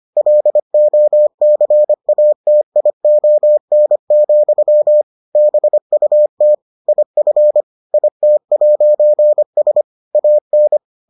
Morse code, heard as a keyed tone, about 25 wpm, 600 Hz. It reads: LOCATION, BUT IF IT'S AN